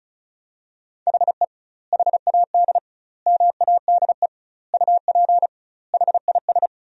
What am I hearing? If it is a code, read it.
Morse code, heard as a keyed tone, about 34 wpm, 700 Hz. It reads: HE HAD MADE UP HIS
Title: 2x Word-Spacing - Sentences from Top 500 Words - 35wpm